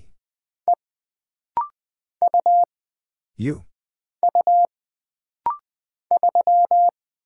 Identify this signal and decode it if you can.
Morse code, heard as a keyed tone, about 20 wpm, 700 Hz. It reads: E U U 3
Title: Single Character (Letter:Number:Punctuation:Prosign) - Rapid-Fire - 20wpm